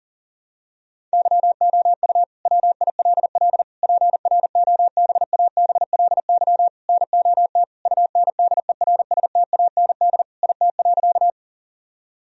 Morse code, heard as a keyed tone, about 40 wpm, 700 Hz. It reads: YOU WILL PROBABLY NOT UNDERSTAND IT.
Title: Sentences from Top 1000 Words - Review Part 4 - 40wpm